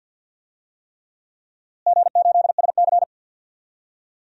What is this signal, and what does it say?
Morse code, heard as a keyed tone, about 52 wpm, 700 Hz. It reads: G8SC
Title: DXCC-100 Call Signs - 4 Characters - Repeated 3x - 50wpm